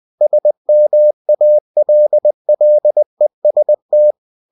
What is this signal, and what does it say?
Morse code, heard as a keyed tone, about 20 wpm, 600 Hz. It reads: SMALLEST